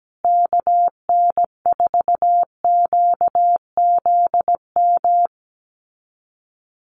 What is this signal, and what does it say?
Morse code, heard as a keyed tone, about 17 wpm, 700 Hz. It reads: KN4QZM
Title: USA Call Signs Repeated 3x 17wpm